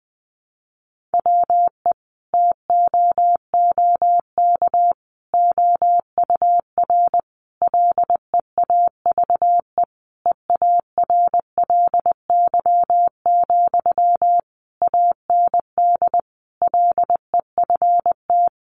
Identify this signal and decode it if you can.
Morse code, heard as a keyed tone, about 19 wpm, 700 Hz. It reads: WE TOOK OUR LEAVE EARLY, AND LEFT
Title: Sentences from Top 400 Words 20wpm